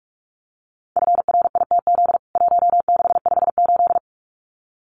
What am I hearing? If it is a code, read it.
Morse code, heard as a keyed tone, about 44 wpm, 700 Hz. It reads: FRITZ 1658
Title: CWT Exchange 45wpm